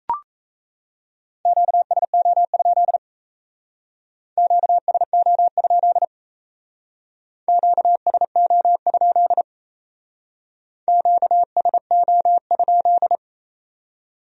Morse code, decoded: QSO? QSO? QSO? QSO?

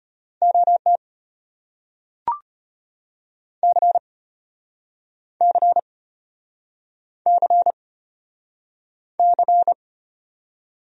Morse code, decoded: OT C C C C

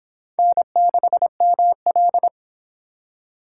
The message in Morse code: N6ML

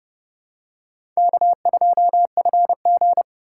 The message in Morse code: K2FG

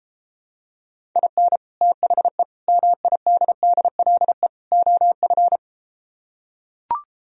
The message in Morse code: IN THE MIDDLE OF

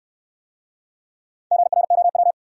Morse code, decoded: GROW

